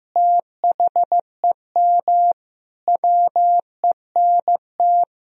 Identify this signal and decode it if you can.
Morse code, heard as a keyed tone, about 15 wpm, 700 Hz. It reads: THEM WENT
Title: Sentences from Top 1000 Words - Review Part 1 - 15wpm